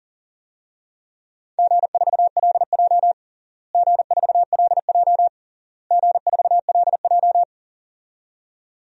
Morse code, heard as a keyed tone, about 40 wpm, 700 Hz. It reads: G4LJ G4LJ G4LJ
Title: DXCC-100 Call Signs - 4 Characters - Repeated 3x - 40wpm